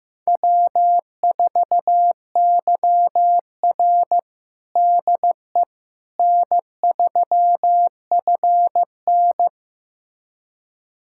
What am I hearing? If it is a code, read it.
Morse code, heard as a keyed tone, about 15 wpm, 700 Hz. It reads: W4YR DE N3FN
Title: USA Callsign DE Callsign - 4 Characters - 15wpm